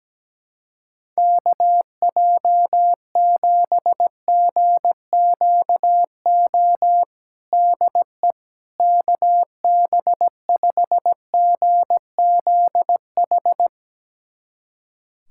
Morse code, heard as a keyed tone, about 17 wpm, 700 Hz. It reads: KJ7GQO DE KB5GZH
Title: USA Callsign DE Callsign - 6 Characters - 17wpm